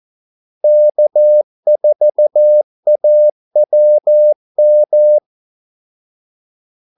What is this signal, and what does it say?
Morse code, heard as a keyed tone, about 14 wpm, 600 Hz. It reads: K4AWM
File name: Callsigns-14wpm